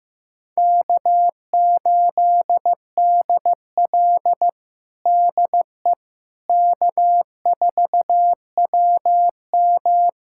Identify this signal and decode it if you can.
Morse code, heard as a keyed tone, about 16 wpm, 700 Hz. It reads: K8DL DE K4WM
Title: USA Callsign DE Callsign - 4 Characters - 15wpm